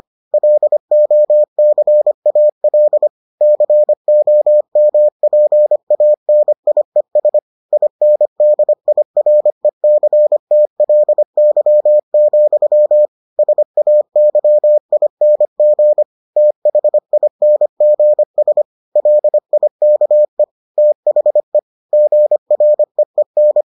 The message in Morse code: LOCAL COMPANIES INDIRECTLY, SAYING THINGS LIKE THE GREEN

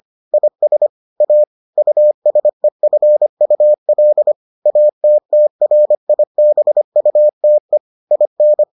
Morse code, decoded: IS A USEFUL ATTRIBUTE IN

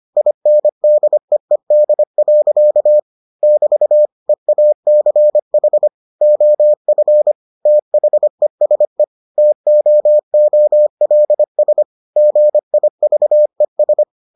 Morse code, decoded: INDEED. <BT> EACH OF THESE TOOLS GIVES

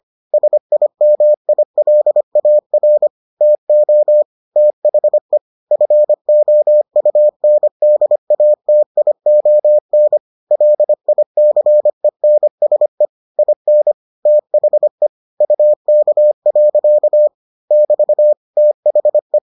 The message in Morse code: SIMILAR TO THE FOUNDATION LICENSE IN THE UK. <BT> THE

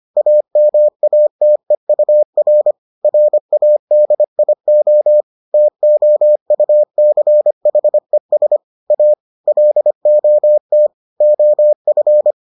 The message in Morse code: AMATEUR RADIO TOUCHES A LOT OF

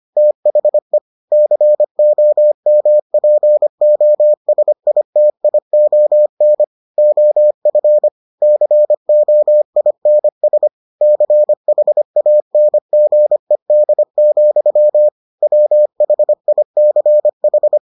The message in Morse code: THE COMPOSITION OF COINS CHANGED, WHICH

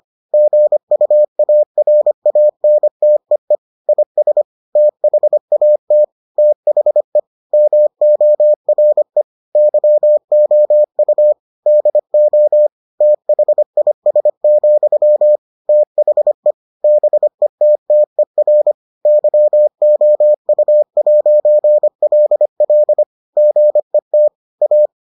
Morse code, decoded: GUARANTEE IS THAT THE MORE YOU DO THIS, THE BETTER YOU'LL GET A

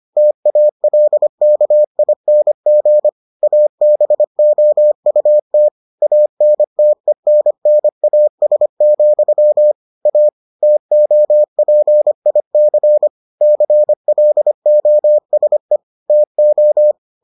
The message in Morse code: TALKING ABOUT ANTENNAS, A TOPIC CLOSE TO

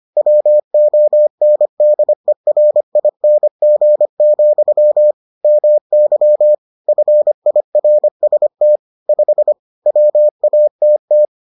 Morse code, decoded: WONDERING, MY FIRST 5 WATT

